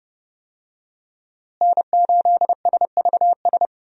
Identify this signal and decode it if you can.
Morse code, heard as a keyed tone, about 32 wpm, 700 Hz. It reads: N8SVS
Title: USA Call Signs - 5 Characters - Repeated 2x - 30wpm